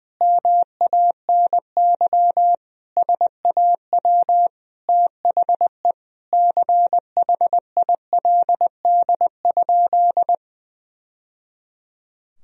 Morse code, heard as a keyed tone, about 21 wpm, 700 Hz. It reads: MANY SAW THE CHILD?
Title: Sentences from Top 1000 Words - Review Part 1 - 20wpm